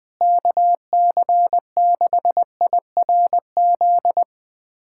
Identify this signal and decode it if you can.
Morse code, heard as a keyed tone, about 20 wpm, 700 Hz. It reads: KC6IRZ
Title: USA Callsign DE Callsign 20wpm